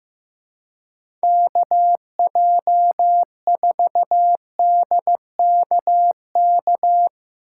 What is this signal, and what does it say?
Morse code, heard as a keyed tone, about 16 wpm, 700 Hz. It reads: KJ4DKK
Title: Speed Racing - USA 2x3 Call Signs 15wpm